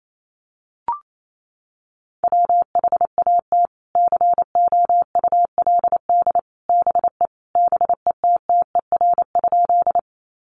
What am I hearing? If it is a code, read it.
Morse code, heard as a keyed tone, about 28 wpm, 700 Hz. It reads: WHAT COULD BE BETTER?
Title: Sentences from Top 400 Words 28wpm